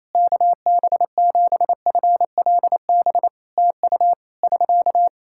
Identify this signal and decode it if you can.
Morse code, heard as a keyed tone, about 27 wpm, 700 Hz. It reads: KB7FLB TU <SK>
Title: Practice QSO - Exchange 4 - Ending - 28wpm